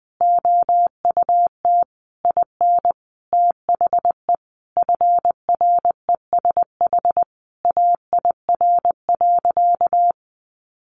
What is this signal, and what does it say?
Morse code, heard as a keyed tone, about 20 wpm, 700 Hz. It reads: OUT IN THE FRESH AIR.
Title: Sentences from Top 1200 Words 20wpm